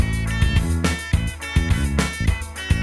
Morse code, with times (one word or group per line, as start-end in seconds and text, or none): none